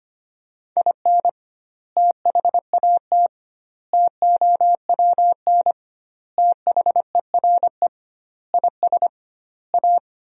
Morse code, heard as a keyed tone, about 25 wpm, 700 Hz. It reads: IN THAT TOWN THERE IS A
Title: 2x Word-Spacing - Sentences from Top 500 Words - 25wpm